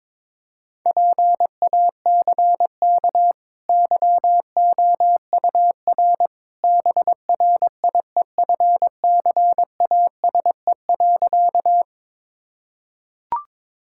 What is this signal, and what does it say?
Morse code, heard as a keyed tone, about 22 wpm, 690 Hz. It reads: PACK YOUR BRIEFCASE.
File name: practice / 22wpm / Easy Sentences with Hints - 22wpm